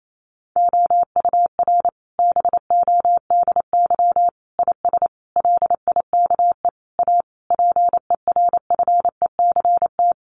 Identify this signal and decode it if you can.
Morse code, heard as a keyed tone, about 28 wpm, 700 Hz. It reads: OUR BODY IS LIKE A PERFECT
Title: Sentences from Top 1000 Words 28wpm